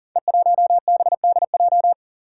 Morse code, heard as a keyed tone, about 39 wpm, 700 Hz. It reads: E1BDJ
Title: Code Groups - 5 Letter:Numbers - 40wpm